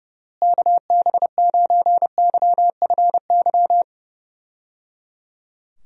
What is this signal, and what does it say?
Morse code, heard as a keyed tone, about 30 wpm, 700 Hz. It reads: KB9YFY
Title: USA Call Signs - 6 Characters - Repeated 3x - 30wpm